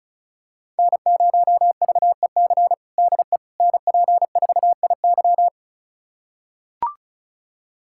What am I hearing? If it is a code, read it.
Morse code, heard as a keyed tone, about 36 wpm, 700 Hz. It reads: N0VEC DE NP4IY
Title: USA Callsign DE Callsign - 5 Characters - 35wpm